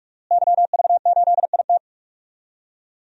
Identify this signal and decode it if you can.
Morse code, heard as a keyed tone, about 46 wpm, 700 Hz. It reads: YV8IT
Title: DXCC-100 Call Signs - 5 Characters - Repeated 3x - 45wpm